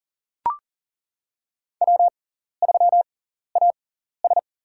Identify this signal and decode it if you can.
Morse code, heard as a keyed tone, about 40 wpm, spaced as for 14 wpm, 700 Hz. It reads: W3AS
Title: USA Call Signs - 4 Characters - 6x Character Spacing - 40wpm